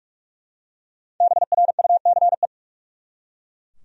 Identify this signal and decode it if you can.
Morse code, heard as a keyed tone, about 43 wpm, 700 Hz. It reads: BRUCE
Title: USA Common Names - 45wpm